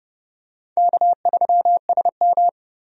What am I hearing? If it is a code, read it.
Morse code, heard as a keyed tone, about 29 wpm, 700 Hz. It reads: K3SM
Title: USA Call Signs - 4 Characters - 30wpm